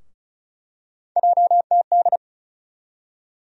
Morse code, heard as a keyed tone, about 36 wpm, 700 Hz. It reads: JTD
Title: USA Call Sign Postfixes - 3 Letters 35wpm